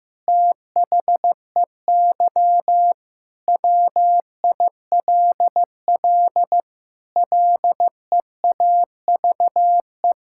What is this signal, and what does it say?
Morse code, heard as a keyed tone, about 15 wpm, 700 Hz. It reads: THEY WILL LEAVE
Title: Sentences from Top 1000 Words - Review Part 6 - 15wpm